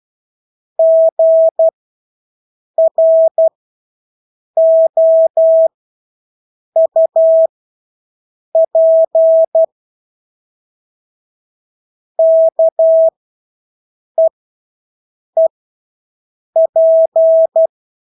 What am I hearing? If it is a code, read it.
Morse code, heard as a keyed tone, about 12 wpm, spaced as for 6 wpm, 650 Hz. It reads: GROUP KEEP
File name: Engelske ord 12-6 wpm